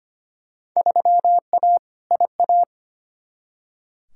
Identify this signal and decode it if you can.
Morse code, heard as a keyed tone, about 25 wpm, 700 Hz. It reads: 3A IA
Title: Field Day Exchanges Encore - 25wpm